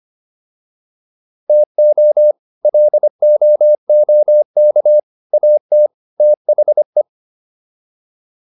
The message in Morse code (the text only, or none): TO LOOK AT THE